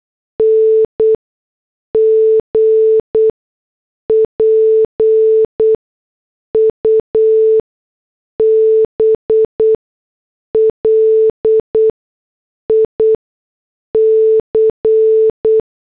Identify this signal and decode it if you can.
Morse code, heard as a keyed tone, about 8 wpm, spaced as for 6 wpm, 440 Hz. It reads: NG PUBLIC